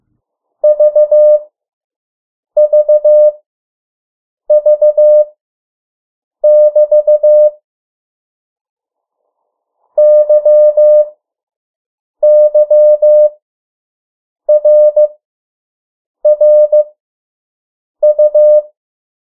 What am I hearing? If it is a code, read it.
Morse code, heard as a keyed tone, about 15 wpm, spaced as for 6 wpm, 600 Hz. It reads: VVV <BT> YYRRU
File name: morse mp3 / 03 urok 2